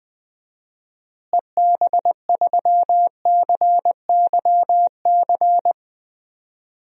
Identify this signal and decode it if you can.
Morse code, heard as a keyed tone, about 20 wpm, 700 Hz. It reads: EB3CYC
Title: DXCC-100 Call Signs - 6 Characters - Repeated 3x - 20wpm